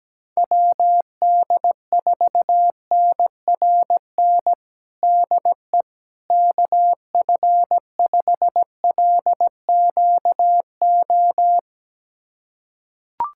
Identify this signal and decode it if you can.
Morse code, heard as a keyed tone, about 17 wpm, 700 Hz. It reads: WD4NRN DE KF5LQO
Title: USA Callsign DE Callsign - 6 Characters - 17wpm